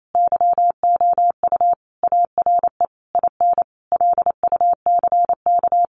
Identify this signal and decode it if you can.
Morse code, heard as a keyed tone, about 28 wpm, 700 Hz. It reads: YOU ARE IN LUCK
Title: Sentences from Top 1500 Words 28wpm